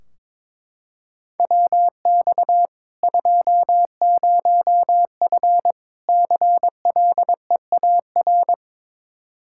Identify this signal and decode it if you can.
Morse code, heard as a keyed tone, about 22 wpm, 700 Hz. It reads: WX 20F CLEAR
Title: K1USN SST QSO Practice 22wpm